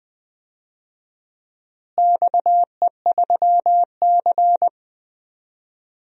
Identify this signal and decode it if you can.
Morse code, heard as a keyed tone, about 20 wpm, 700 Hz. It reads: XE3C
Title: DXCC-100 Call Signs - 4 Characters - Repeated 2x - 20wpm